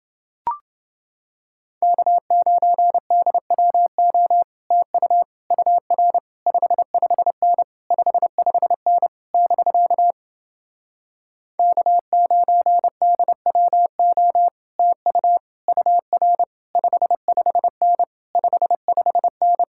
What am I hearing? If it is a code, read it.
Morse code, heard as a keyed tone, about 30 wpm, 700 Hz. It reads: K9DWO TU UR 55N 55N <BK> K9DWO TU UR 55N 55N